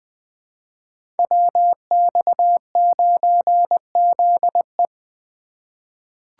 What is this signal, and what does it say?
Morse code, heard as a keyed tone, about 20 wpm, 700 Hz. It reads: WX9ZE
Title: Code Groups - 5 Letter:Numbers - 20wpm